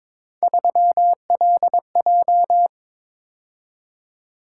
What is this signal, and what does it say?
Morse code, heard as a keyed tone, about 22 wpm, 700 Hz. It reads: 3LJ
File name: practice / 22wpm / Three Letter-Number Combinations - 22wpm